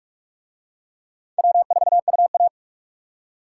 Morse code, heard as a keyed tone, about 42 wpm, 700 Hz. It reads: W4UA